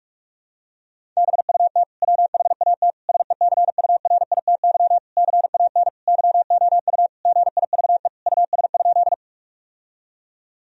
Morse code, heard as a keyed tone, about 41 wpm, 700 Hz. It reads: BUT WHAT SECURITY CAN YOU GIVE US?
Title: Sentences from Top 1000 Words - Review Part 4 - 45wpm